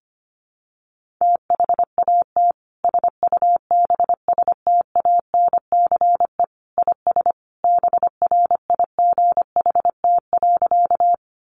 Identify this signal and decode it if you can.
Morse code, heard as a keyed tone, about 25 wpm, 700 Hz. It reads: THAT SUBSTANCE IS BRIGHT.
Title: Sentences from Top 1500 Words 25wpm